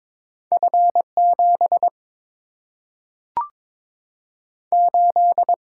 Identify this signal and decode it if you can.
Morse code, heard as a keyed tone, about 22 wpm, 700 Hz. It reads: F 7 8
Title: Single-Letter Single-Number 22wpm